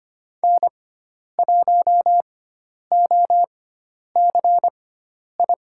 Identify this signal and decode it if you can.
Morse code, heard as a keyed tone, about 25 wpm, spaced as for 10 wpm, 700 Hz. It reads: N1OCI
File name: Lesson 16g - Letter c - Call Signs 25wpm